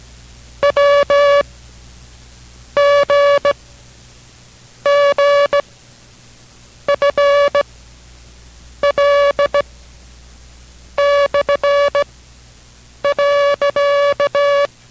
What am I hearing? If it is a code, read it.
Morse code, heard as a keyed tone, about 17 wpm, spaced as for 6 wpm, 570 Hz. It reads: WGGFL/.